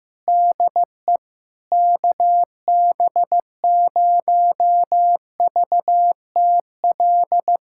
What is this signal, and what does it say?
Morse code, heard as a keyed tone, about 15 wpm, 700 Hz. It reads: DE KB0VTL
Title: USA Callsign DE Callsign 15wpm